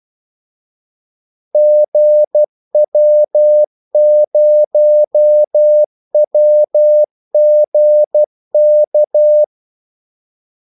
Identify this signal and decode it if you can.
Morse code, heard as a keyed tone, about 12 wpm, 600 Hz. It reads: GW0WGK
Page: 2 ekstra ordmellemrum mellem kaldesignalerne